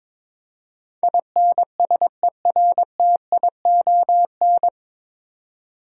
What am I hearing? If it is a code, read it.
Morse code, heard as a keyed tone, about 22 wpm, 700 Hz. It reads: INSERTION
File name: 9 Letter Words - First 3 Letters Given - 22wpm